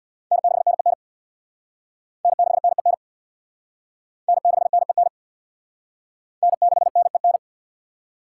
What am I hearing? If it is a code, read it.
Morse code, heard as a keyed tone, about 60 wpm, 700 Hz. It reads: N6NEN N6NEN N6NEN N6NEN
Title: Speed Racing - USA 1x3 Call Signs 50wpm